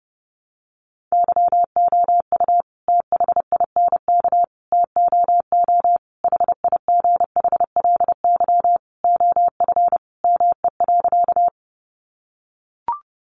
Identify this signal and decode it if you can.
Morse code, heard as a keyed tone, about 30 wpm, 700 Hz. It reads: YOU THINK TOO HIGHLY OF ME.
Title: Sentences from Top 1100 Words 30wpm